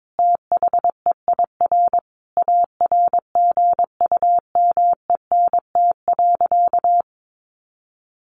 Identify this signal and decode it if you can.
Morse code, heard as a keyed tone, about 22 wpm, 700 Hz. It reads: THEIR ARGUMENT.
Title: Sentences from Top 1500 Words 22wpm